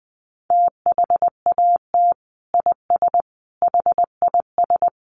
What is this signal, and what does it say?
Morse code, heard as a keyed tone, about 20 wpm, 700 Hz. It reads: THAT IS HIS
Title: Sentences from Top 900 Words 20wpm